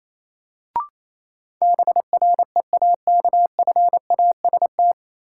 Morse code, heard as a keyed tone, about 28 wpm, 700 Hz. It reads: BREAKFAST
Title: Sentences from Top 1500 Words - New Word Drill Alt - 28wpm